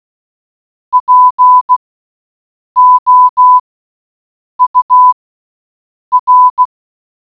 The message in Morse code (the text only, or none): POUR